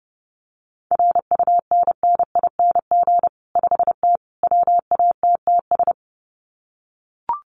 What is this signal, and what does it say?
Morse code, heard as a keyed tone, about 30 wpm, 700 Hz. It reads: RUNNING 5T WATTS
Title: Common QSO Phrases 30wpm